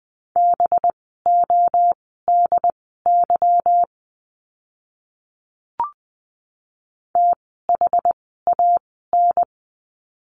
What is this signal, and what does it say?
Morse code, heard as a keyed tone, about 20 wpm, spaced as for 14 wpm, 700 Hz. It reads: BODY THAN
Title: 185 Lesson 30f - Letter b - Words 20wpm